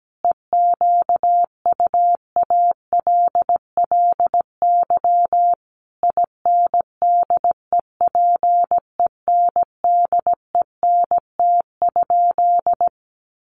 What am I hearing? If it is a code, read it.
Morse code, heard as a keyed tone, about 17 wpm, 700 Hz. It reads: EQUALLY INDEPENDENT?
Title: Sentences from Top 1400 Words 17wpm